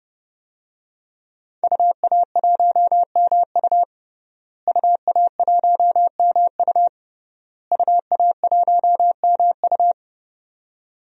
Morse code, heard as a keyed tone, about 30 wpm, 700 Hz. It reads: UA1MU UA1MU UA1MU
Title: DXCC-100 Call Signs - 5 Characters - Repeated 3x - 30wpm